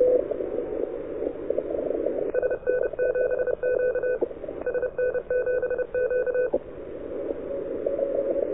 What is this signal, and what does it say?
Morse code, heard as a keyed tone, about 31 wpm, 1500 Hz, with radio noise. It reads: SN7Q SN7Q
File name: audio zawody / CQWW 2014 CW SN7Q